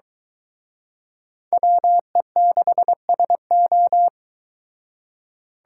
Morse code, heard as a keyed tone, about 23 wpm, 700 Hz. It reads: WE6SO